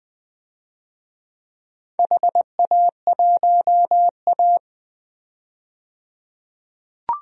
Morse code, heard as a keyed tone, about 20 wpm, 700 Hz. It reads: HA1A
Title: DXCC-100 Call Signs - 4 Characters - 20wpm